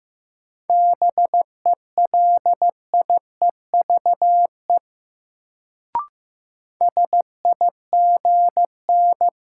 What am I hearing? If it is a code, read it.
Morse code, heard as a keyed tone, about 15 wpm, 700 Hz. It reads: BELIEVE SIGN
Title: Sentences from Top 1000 Words - New Word Drill - 15wpm